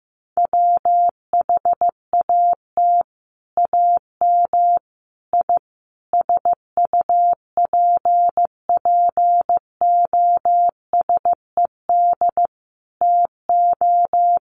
Morse code, heard as a keyed tone, about 15 wpm, 700 Hz. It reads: WHAT AM I SUPPOSED TO
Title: Sets of 5 Words Encore2 15wpm